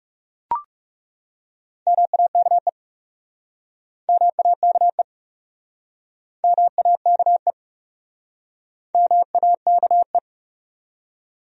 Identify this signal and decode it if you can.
Morse code, heard as a keyed tone, about 37 wpm, 700 Hz. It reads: MAKE MAKE MAKE MAKE
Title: Speed-Racing Top 100 Words 30wpm